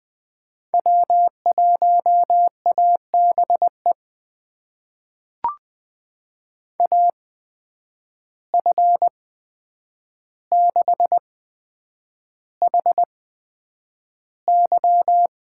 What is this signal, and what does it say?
Morse code, heard as a keyed tone, about 20 wpm, spaced as for 9 wpm, 700 Hz. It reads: W1ABE AF6HY
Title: USA Call Signs - 5 Characters - 8x Character Spacing - 20wpm